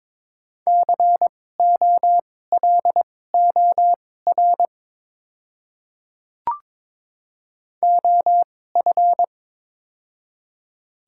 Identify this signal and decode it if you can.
Morse code, heard as a keyed tone, about 22 wpm, spaced as for 16 wpm, 700 Hz. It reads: COLOR OF
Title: 2x Character Spacing - Top 500 Words - 22wpm